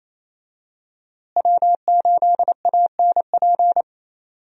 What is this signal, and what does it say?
Morse code, heard as a keyed tone, about 27 wpm, 700 Hz. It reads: W8ANP
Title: USA Call Signs - 5 Characters - Repeated 2x - 28wpm